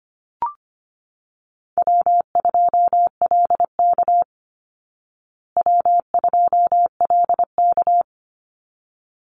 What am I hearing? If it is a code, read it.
Morse code, heard as a keyed tone, about 25 wpm, 700 Hz. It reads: W2LK W2LK
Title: Easy Call Signs Repeated 2x - 25wpm